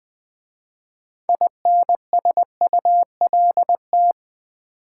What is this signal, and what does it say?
Morse code, heard as a keyed tone, about 20 wpm, 700 Hz. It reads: INSULT
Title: 6 Letter Words - First 3 Letters Given - 20wpm